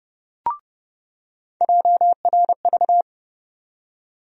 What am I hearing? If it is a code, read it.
Morse code, heard as a keyed tone, about 29 wpm, 700 Hz. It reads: JRV
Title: USA Call Sign Postfixes - 3 Letters 30wpm